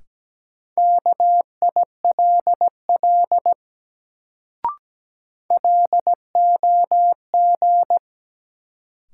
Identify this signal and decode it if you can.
Morse code, heard as a keyed tone, about 17 wpm, 700 Hz. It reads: KILL LOG
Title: Sentences from Top 1400 Words - New Word Drill - 17wpm